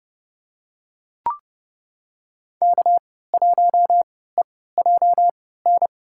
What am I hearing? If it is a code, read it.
Morse code, heard as a keyed tone, about 32 wpm, spaced as for 17 wpm, 700 Hz. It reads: K1EJN
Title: USA Call Signs - 5 Characters - 3x Character Spacing - 30wpm